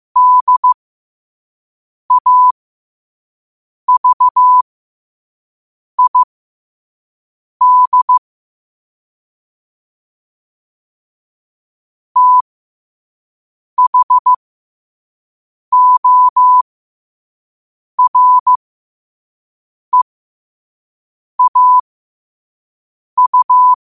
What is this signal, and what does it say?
Morse code, heard as a keyed tone, about 15 wpm, spaced as for 5 wpm, 1000 Hz. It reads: DAVID THOREAU